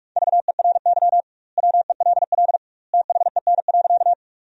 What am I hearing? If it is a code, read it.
Morse code, heard as a keyed tone, about 41 wpm, 700 Hz. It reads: VERY WELL THEN.
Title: 2x Word-Spacing - Sentences from Top 200 Words - 45wpm